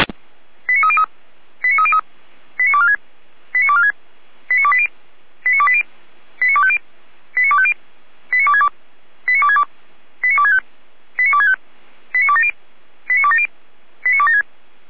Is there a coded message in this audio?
Yes: five-tone selective call (ZVEI) from 0.7 to 14.4 s, heard as quick runs of five tones, tones of about 70 ms. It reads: 89303 89303 89246 89246 89280 89280 89250 89250 89262 89262 89265 89265 89270 89270 89276